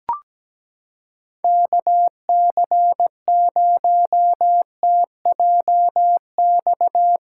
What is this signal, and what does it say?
Morse code, heard as a keyed tone, about 17 wpm, 700 Hz. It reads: KC0TJX
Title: USA Callsign DE Callsign 17wpm